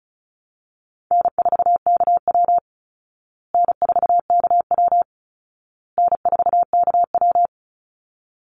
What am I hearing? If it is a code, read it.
Morse code, heard as a keyed tone, about 34 wpm, 700 Hz. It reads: N4KW N4KW N4KW
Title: Easy Call Signs Repeated 3x - 35wpm